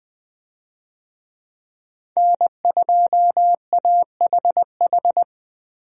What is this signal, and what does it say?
Morse code, heard as a keyed tone, about 20 wpm, 700 Hz. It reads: N2AHH
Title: USA Call Signs - 5 Characters - 20wpm